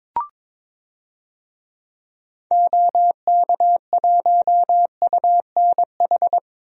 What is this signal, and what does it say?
Morse code, heard as a keyed tone, about 22 wpm, 700 Hz. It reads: OK1UNH
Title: DXCC-100 Call Signs - 6 Characters - 22wpm